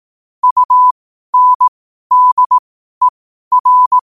Morse code, heard as a keyed tone, about 18 wpm, spaced as for 12 wpm, 1000 Hz. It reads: UNDER
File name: qotdfiles / 12wpm